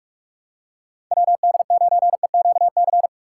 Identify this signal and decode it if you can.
Morse code, heard as a keyed tone, about 42 wpm, 700 Hz. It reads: WD9EQC
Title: USA Call Signs - 6 Characters - 6x Character Spacing - 45wpm